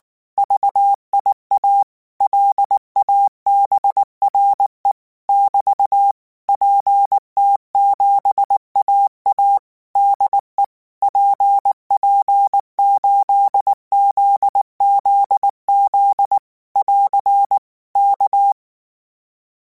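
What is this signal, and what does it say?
Morse code, heard as a keyed tone, about 19 wpm, 800 Hz. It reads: VIA LABRE <BT> PT7AA DE PP8ZZZ <AR> K
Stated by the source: Morse code practice text at 18 wpm, from Portal do CW.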